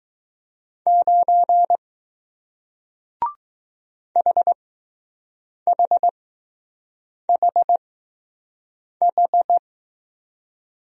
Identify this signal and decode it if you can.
Morse code, heard as a keyed tone, about 21 wpm, spaced as for 6 wpm, 700 Hz. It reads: 9 HHHH